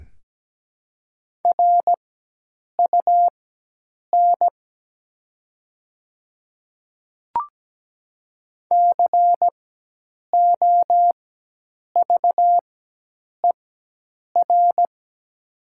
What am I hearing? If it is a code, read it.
Morse code, heard as a keyed tone, about 17 wpm, spaced as for 8 wpm, 700 Hz. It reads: RUN COVER
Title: 4x Character Spacing - Top 300 Words - 17wpm